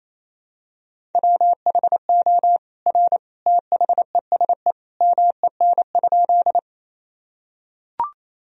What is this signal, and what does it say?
Morse code, heard as a keyed tone, about 28 wpm, 700 Hz. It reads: WHO R THESE MEN?
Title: Sentences from Top 200 Words and Abbreviations 28wpm